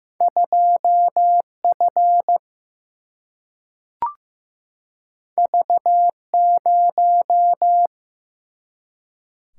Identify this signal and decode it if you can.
Morse code, heard as a keyed tone, about 15 wpm, 700 Hz. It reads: 2F V0
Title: Single-Letter Single-Number 15wpm